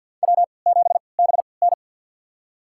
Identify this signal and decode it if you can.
Morse code, heard as a keyed tone, about 47 wpm, spaced as for 28 wpm, 700 Hz. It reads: W7BN
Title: USA Call Signs - 4 Characters - 3x Character Spacing - 50wpm